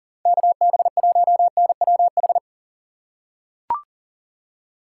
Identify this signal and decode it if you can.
Morse code, heard as a keyed tone, about 40 wpm, 700 Hz. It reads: KB1NWH
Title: USA Call Signs - 6 Characters - 2x Character Spacing - 40wpm